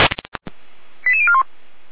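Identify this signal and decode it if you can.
Five-tone selective call (ZVEI), heard as quick runs of five tones, tones of about 70 ms. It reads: 88041
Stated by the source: ZVEI five-tone selective call received on an SDR.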